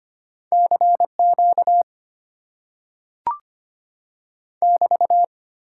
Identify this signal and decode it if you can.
Morse code, heard as a keyed tone, about 25 wpm, 690 Hz. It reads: CQ <BT>
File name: Top 90 QSO Elements - 8x Character Spacing - 25wpm